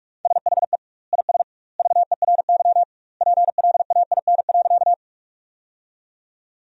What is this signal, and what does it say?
Morse code, heard as a keyed tone, about 42 wpm, 700 Hz. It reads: SHE IS VERY PLAIN.
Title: 2x Word-Spacing - Sentences from Top 400 Words - 45wpm